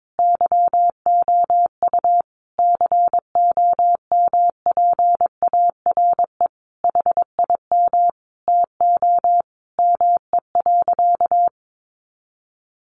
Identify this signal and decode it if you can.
Morse code, heard as a keyed tone, about 22 wpm, 700 Hz. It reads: YOU COMPARE HIM TO ME.
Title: Sentences from Top 1300 Words 22wpm